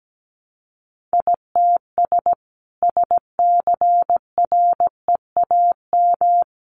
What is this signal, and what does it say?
Morse code, heard as a keyed tone, about 17 wpm, 700 Hz. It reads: ITS SCREAM
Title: A Sound Of Thunder 17wpm